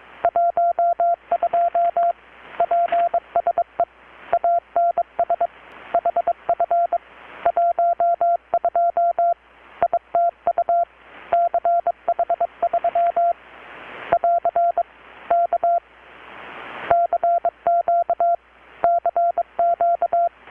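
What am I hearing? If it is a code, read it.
Morse code, heard as a keyed tone, about 22 wpm, 690 Hz. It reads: 12 PSE ANS HF 12 ITU CH3 <AR> K CQ CQ
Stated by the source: Morse from KPH's Night of Nights XX maritime radiotelegraphy event, received in Chiapas, Mexico about 3,800 km away on a Kenwood R-600 with a grounded 10 m long-wire antenna.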